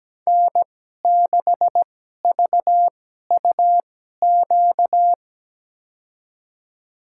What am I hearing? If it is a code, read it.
Morse code, heard as a keyed tone, about 17 wpm, spaced as for 12 wpm, 700 Hz. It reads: N6VUQ
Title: USA Call Signs - 5 Characters - 2x Character Spacing - 17wpm